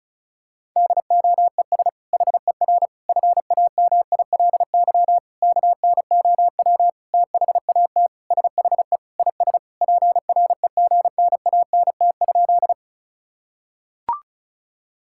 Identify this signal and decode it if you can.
Morse code, heard as a keyed tone, about 34 wpm, 700 Hz. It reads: DOES HER FAMILY KNOW THAT SHE IS PREGNANT?
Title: Sentences from Top 1000 Words - Review Part 6 - 35wpm